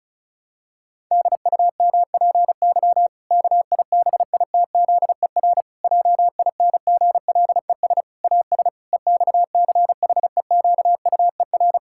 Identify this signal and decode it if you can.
Morse code, heard as a keyed tone, about 36 wpm, 700 Hz. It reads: DUMPY KIBITZER JINGLES AS EXCHEQUER
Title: Pangram Challenge v2 - 35wpm